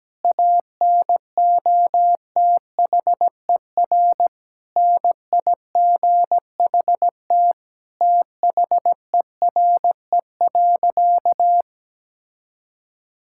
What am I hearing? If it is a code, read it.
Morse code, heard as a keyed tone, about 17 wpm, 700 Hz. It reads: ANOTHER NIGHT THERE.
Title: Sentences from Top 1000 Words - Review Part 5 - 17wpm